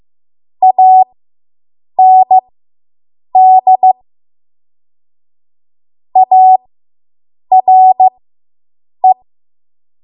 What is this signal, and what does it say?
Morse code, heard as a keyed tone, about 15 wpm, spaced as for 7 wpm, 750 Hz. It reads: AND ARE